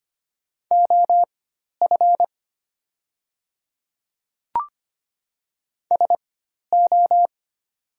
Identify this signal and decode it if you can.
Morse code, heard as a keyed tone, about 25 wpm, spaced as for 12 wpm, 700 Hz. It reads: OF SO
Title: 4x Character Spacing - Top 300 Words - 25wpm